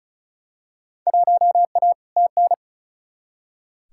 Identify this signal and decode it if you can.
Morse code, heard as a keyed tone, about 36 wpm, 700 Hz. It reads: 1A TN
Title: Field Day Exchanges - 35wpm